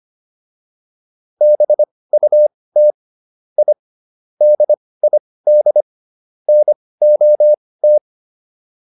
Morse code, heard as a keyed tone, about 25 wpm, spaced as for 18 wpm, 600 Hz. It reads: BUT I DID NOT